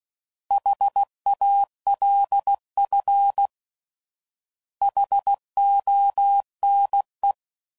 Morse code, heard as a keyed tone, about 16 wpm, 800 Hz. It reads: HALF HONE